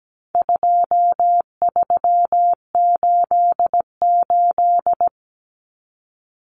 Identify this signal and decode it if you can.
Morse code, heard as a keyed tone, about 17 wpm, 700 Hz. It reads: 2388
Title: Numbers - 4-Digit 17wpm